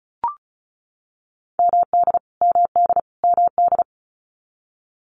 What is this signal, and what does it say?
Morse code, heard as a keyed tone, about 34 wpm, 700 Hz. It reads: MD MD MD
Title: US States Repeated 3x 35wpm